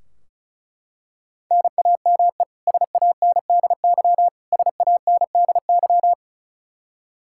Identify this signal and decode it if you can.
Morse code, heard as a keyed tone, about 36 wpm, 700 Hz. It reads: NAME SANDY SANDY
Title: Practice QSO - Exchange 1 - Beginning - 35wpm